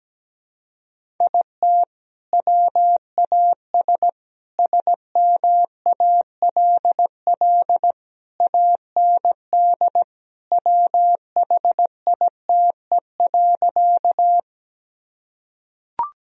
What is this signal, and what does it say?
Morse code, heard as a keyed tone, about 17 wpm, 700 Hz. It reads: IT WAS SMALL AND WHITE.
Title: Sentences from Top 1000 Words - Review Part 1 - 17wpm